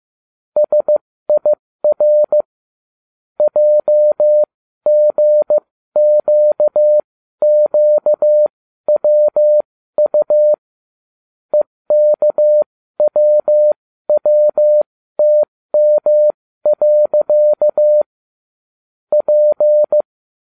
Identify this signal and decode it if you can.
Morse code, heard as a keyed tone, about 15 wpm, spaced as for 13 wpm, 600 Hz. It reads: SIR JGQQWU EKWWTM. P